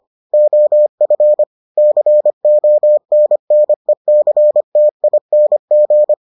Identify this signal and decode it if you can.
Morse code, heard as a keyed tone, about 25 wpm, 600 Hz. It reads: OF CONNECTING